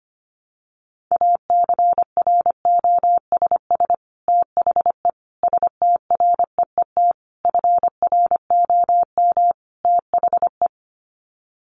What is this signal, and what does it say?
Morse code, heard as a keyed tone, about 25 wpm, 700 Hz. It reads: ACROSS THE STREET FROM THE
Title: Sets of 5 Words Encore 25wpm